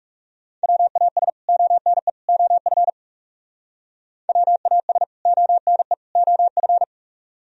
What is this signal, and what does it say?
Morse code, heard as a keyed tone, about 37 wpm, 700 Hz. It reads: WAS ONE OF WAS ONE OF
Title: Speed-Racing - Sets of 3 Words 30wpm